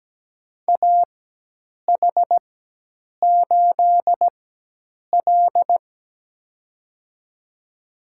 Morse code, heard as a keyed tone, about 17 wpm, spaced as for 8 wpm, 700 Hz. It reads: AH8L
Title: USA Call Signs - 4 Characters - 4x Character Spacing - 17wpm